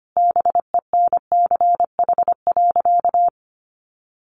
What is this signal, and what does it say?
Morse code, heard as a keyed tone, about 25 wpm, 700 Hz. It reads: BENCH.
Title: Sentences from Top 1400 Words 25wpm